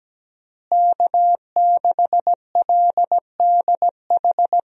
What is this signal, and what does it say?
Morse code, heard as a keyed tone, about 17 wpm, 700 Hz. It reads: K6LDH
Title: USA Call Signs - 5 Characters - Repeated 2x - 17wpm